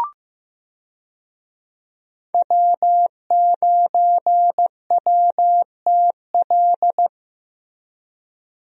Morse code, heard as a keyed tone, about 15 wpm, 700 Hz. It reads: W9WTL
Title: USA Call Signs - 5 Characters - Repeated 2x - 15wpm